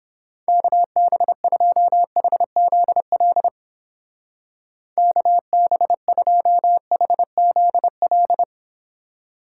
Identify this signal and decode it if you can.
Morse code, heard as a keyed tone, about 31 wpm, 700 Hz. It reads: KB2HZL KB2HZL